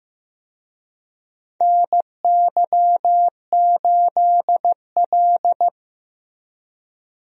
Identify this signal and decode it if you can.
Morse code, heard as a keyed tone, about 16 wpm, 700 Hz. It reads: NY8L
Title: USA Call Signs - 4 Characters - Repeated 2x - 15wpm